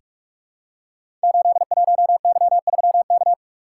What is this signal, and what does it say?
Morse code, heard as a keyed tone, about 44 wpm, 700 Hz. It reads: 81Y3K
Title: Code Groups - 5 Letter:Numbers - 45wpm